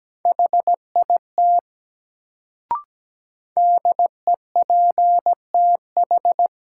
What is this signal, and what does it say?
Morse code, heard as a keyed tone, about 17 wpm, 700 Hz. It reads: HIT DEPTH
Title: Sentences from Top 1200 Words - New Word Drill Alt - 17wpm